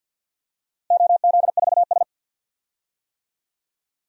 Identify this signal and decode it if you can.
Morse code, heard as a keyed tone, about 51 wpm, 700 Hz. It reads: OZ4S
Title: DXCC-100 Call Signs - 4 Characters - Repeated 3x - 50wpm